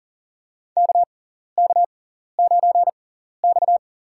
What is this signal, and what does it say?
Morse code, heard as a keyed tone, about 40 wpm, spaced as for 14 wpm, 700 Hz. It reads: KK9X
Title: USA Call Signs - 4 Characters - 6x Character Spacing - 40wpm